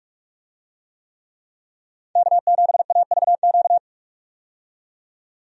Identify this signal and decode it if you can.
Morse code, heard as a keyed tone, about 46 wpm, 700 Hz. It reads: K7AVQ
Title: USA Call Signs - 5 Characters - 45wpm